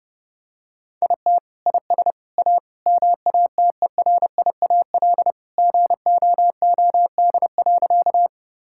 Morse code, heard as a keyed tone, about 28 wpm, 700 Hz. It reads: IT IS A MATERIAL GOOD.
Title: Sentences from Top 1000 Words - Review Part 5 - 30wpm